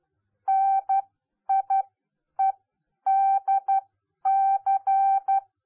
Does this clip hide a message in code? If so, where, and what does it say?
Yes: Morse code from 0.5 to 5.4 s, heard as a keyed tone, about 12 wpm, spaced as for 9 wpm, 780 Hz. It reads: NIEDC